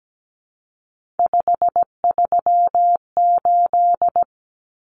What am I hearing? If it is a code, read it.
Morse code, heard as a keyed tone, about 17 wpm, 700 Hz. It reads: 538
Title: Numbers - 3-Digit 17wpm